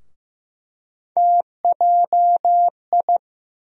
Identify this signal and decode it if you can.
Morse code, heard as a keyed tone, about 15 wpm, 700 Hz. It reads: TJI
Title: USA Call Sign Postfixes - 3 Letters 15wpm